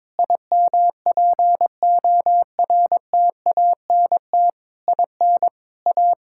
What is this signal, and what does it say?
Morse code, heard as a keyed tone, about 22 wpm, 700 Hz. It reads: IMPORTANT IN A
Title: Sentences from Top 1000 Words - Review Part 5 - 22wpm